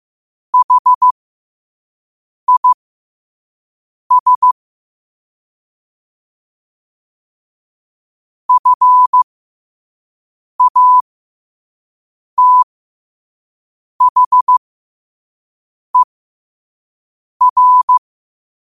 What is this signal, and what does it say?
Morse code, heard as a keyed tone, about 15 wpm, spaced as for 5 wpm, 1000 Hz. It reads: HIS FATHER
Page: learn-to-code